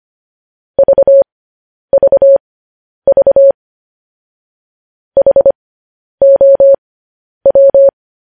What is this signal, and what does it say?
Morse code, heard as a keyed tone, about 25 wpm, spaced as for 10 wpm, 550 Hz. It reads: VVV HOW
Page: news-headlines-in-morse-code-at-10-wpm